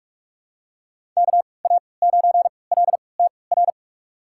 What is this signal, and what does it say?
Morse code, heard as a keyed tone, about 46 wpm, spaced as for 26 wpm, 700 Hz. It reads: KA9LTR
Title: USA Call Signs - 6 Characters - 3x Character Spacing - 45wpm